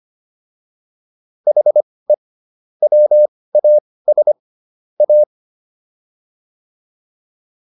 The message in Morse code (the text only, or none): HE WAS A